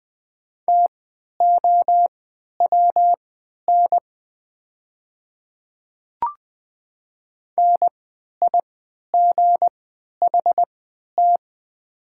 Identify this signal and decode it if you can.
Morse code, heard as a keyed tone, about 20 wpm, spaced as for 11 wpm, 700 Hz. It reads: TOWN NIGHT
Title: 3x Character Spacing - Top 500 Words - 20wpm